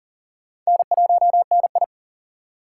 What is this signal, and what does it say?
Morse code, heard as a keyed tone, about 39 wpm, 700 Hz. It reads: N1NI
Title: USA Call Signs - 4 Characters - 6x Character Spacing - 40wpm